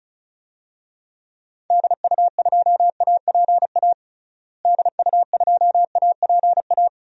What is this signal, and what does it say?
Morse code, heard as a keyed tone, about 36 wpm, 700 Hz. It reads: DU2APA DU2APA
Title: DXCC-100 Call Signs - 6 Characters - Repeated 3x - 35wpm